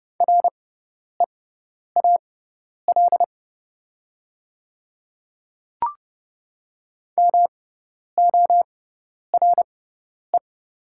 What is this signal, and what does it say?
Morse code, heard as a keyed tone, about 30 wpm, spaced as for 10 wpm, 700 Hz. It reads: REAL MORE
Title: 6x Character Spacing - Top 300 Words - 30wpm